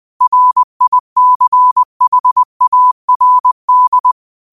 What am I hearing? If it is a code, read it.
Morse code, heard as a keyed tone, about 20 wpm, 1000 Hz. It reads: RICHARD
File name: qotdfiles / 20wpm